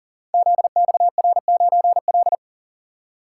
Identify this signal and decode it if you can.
Morse code, heard as a keyed tone, about 42 wpm, 700 Hz. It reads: ZXR9L
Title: Code Groups - 5 Letter:Numbers - 40wpm